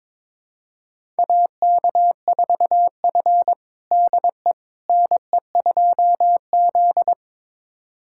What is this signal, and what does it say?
Morse code, heard as a keyed tone, about 22 wpm, 700 Hz. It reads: AK4F DE NE2Z
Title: USA Callsign DE Callsign - 4 Characters - 22wpm